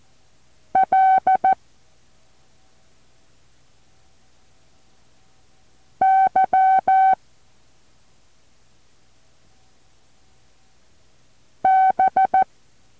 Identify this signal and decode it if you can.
Morse code, heard as a keyed tone, about 14 wpm, 760 Hz. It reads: L Y B
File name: intl morse code2